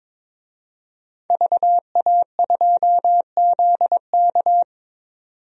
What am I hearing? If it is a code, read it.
Morse code, heard as a keyed tone, about 22 wpm, 700 Hz. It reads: VA2ZK
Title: DXCC-100 Call Signs - 5 Characters - 22wpm